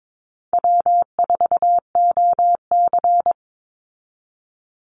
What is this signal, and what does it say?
Morse code, heard as a keyed tone, about 22 wpm, 700 Hz. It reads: W4OC
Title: Easy Call Signs Repeated 3x - 22wpm